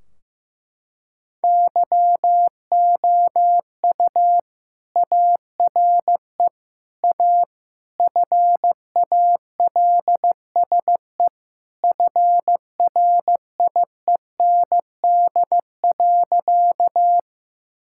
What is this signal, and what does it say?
Morse code, heard as a keyed tone, about 15 wpm, 700 Hz. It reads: YOU ARE A FALSE FRIEND.
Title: Sentences from Top 1600 Words 15wpm